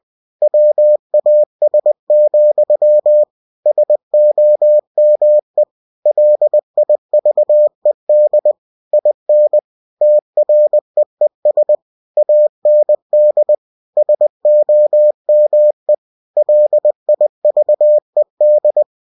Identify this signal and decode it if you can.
Morse code, heard as a keyed tone, about 20 wpm, 600 Hz. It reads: WAS, SOME LIVED IN TREES AND SOME LIVED
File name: cw-books / winnie-the-pooh / winnie-20wpm-page-0018